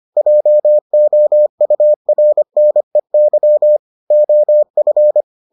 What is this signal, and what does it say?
Morse code, heard as a keyed tone, about 25 wpm, 600 Hz. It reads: JOURNEY OF